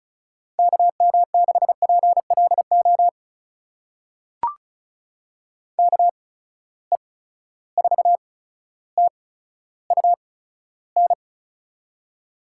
Morse code, heard as a keyed tone, about 36 wpm, spaced as for 15 wpm, 700 Hz. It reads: KM6PLO KE4TUN
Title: USA Call Signs - 6 Characters - 8x Character Spacing - 35wpm